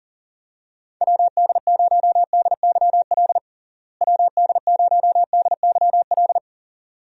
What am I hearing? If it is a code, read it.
Morse code, heard as a keyed tone, about 40 wpm, 700 Hz. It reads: WD0DYL WD0DYL
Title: USA Call Signs - 6 Characters - Repeated 2x - 40wpm